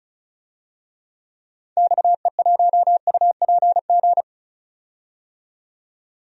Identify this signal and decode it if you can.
Morse code, heard as a keyed tone, about 36 wpm, 700 Hz. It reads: XE1UPG
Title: DXCC-100 Call Signs - 6 Characters - 35wpm